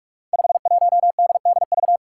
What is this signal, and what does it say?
Morse code, heard as a keyed tone, about 44 wpm, 700 Hz. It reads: 51DDV
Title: Five Challenging Letter-Number Combinations 45wpm